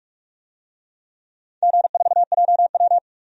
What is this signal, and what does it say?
Morse code, heard as a keyed tone, about 46 wpm, 700 Hz. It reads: G4JW
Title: DXCC-100 Call Signs - 4 Characters - 45wpm